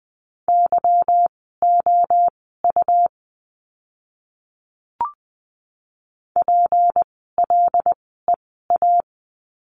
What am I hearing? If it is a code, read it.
Morse code, heard as a keyed tone, about 20 wpm, spaced as for 14 wpm, 700 Hz. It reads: YOU PLEA
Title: 153 Lesson 25f - Letter p - Words 20wpm